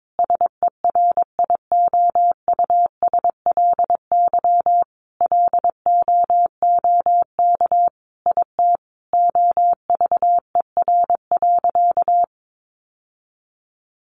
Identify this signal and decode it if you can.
Morse code, heard as a keyed tone, about 22 wpm, 700 Hz. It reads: SERIOUSLY LOOK IT OVER.
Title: Sentences from Top 1200 Words 22wpm